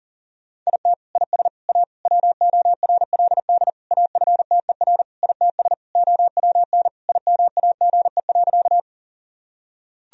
Morse code, heard as a keyed tone, about 40 wpm, 700 Hz. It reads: IT IS A WORLD AFTER ITS OWN IMAGE.
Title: Sentences from Top 1000 Words - Review Part 4 - 40wpm